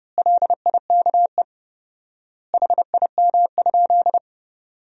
Morse code, heard as a keyed tone, about 30 wpm, 700 Hz. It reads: LIKE HIM?
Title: 4x Word-Spacing - Sentences from Top 100 Words - 30wpm